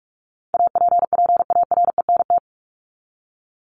Morse code, heard as a keyed tone, about 44 wpm, 700 Hz. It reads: APPARENT
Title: LY Suffix - Top 100 Words with Base Words 45wpm